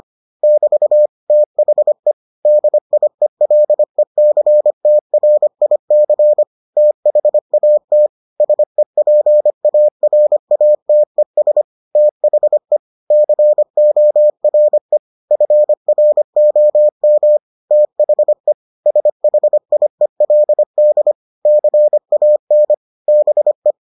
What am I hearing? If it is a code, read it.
Morse code, heard as a keyed tone, about 25 wpm, 600 Hz. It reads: <BT> THE DIELECTRIC THAT SEPARATES THE CORE FROM THE SHIELD CAN BE